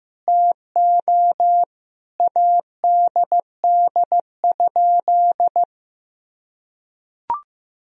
Morse code, heard as a keyed tone, about 15 wpm, 700 Hz. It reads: TO ADD?
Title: Sentences from Top 1000 Words - Review Part 1 - 15wpm